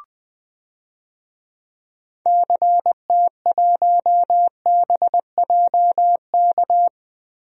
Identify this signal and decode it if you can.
Morse code, heard as a keyed tone, about 20 wpm, 700 Hz. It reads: CT1BJK
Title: DXCC-100 Call Signs - 6 Characters - Repeated 2x - 20wpm